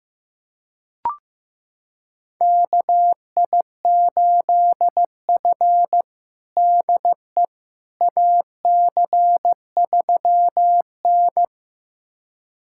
Morse code, heard as a keyed tone, about 15 wpm, 700 Hz. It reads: KI8F DE AC3N
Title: USA Callsign DE Callsign - 4 Characters - 15wpm